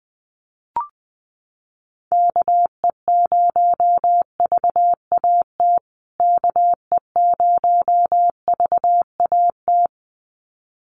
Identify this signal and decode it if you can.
Morse code, heard as a keyed tone, about 20 wpm, 700 Hz. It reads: KE0VAT KE0VAT
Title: USA Call Signs Repeated 2x 20wpm